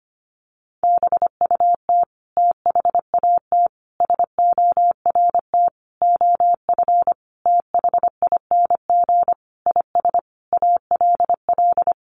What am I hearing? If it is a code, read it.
Morse code, heard as a keyed tone, about 25 wpm, 700 Hz. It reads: BUT THAT SORT OF THING IS ALL
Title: Sentences from Top 1300 Words 25wpm